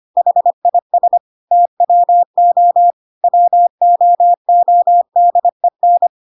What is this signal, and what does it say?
Morse code, heard as a keyed tone, about 25 wpm, 700 Hz. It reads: HIS TWO WOODEN
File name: Pinocchio / Pinocchio ch3 25 25 ik1hge